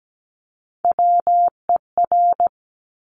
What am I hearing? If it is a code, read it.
Morse code, heard as a keyed tone, about 17 wpm, 700 Hz. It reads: WER
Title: Top Three Letter Combinations in Words 17wpm